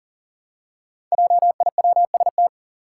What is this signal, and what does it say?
Morse code, heard as a keyed tone, about 40 wpm, 700 Hz. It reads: JIWST
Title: Five Challenging Letter-Number Combinations 40wpm